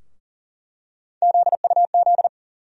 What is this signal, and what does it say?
Morse code, heard as a keyed tone, about 40 wpm, 700 Hz. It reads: ZUZ
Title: USA Call Sign Postfixes - 3 Letters 40wpm